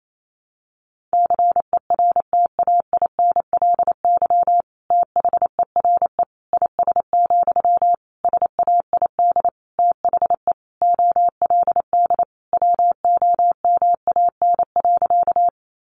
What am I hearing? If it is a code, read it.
Morse code, heard as a keyed tone, about 28 wpm, 700 Hz. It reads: CERTAINLY THERE IS, SAID THE OLD WOMAN.
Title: Sentences from Top 1100 Words 28wpm